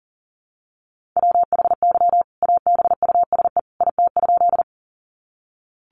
Morse code, heard as a keyed tone, about 40 wpm, 700 Hz. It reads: WHY ABUSE IT?
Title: Sentences from Top 1300 Words 40wpm